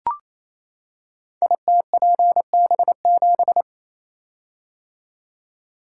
Morse code, heard as a keyed tone, about 28 wpm, 700 Hz. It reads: ITPB7
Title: Code Groups - 5 Letter:Numbers - 28wpm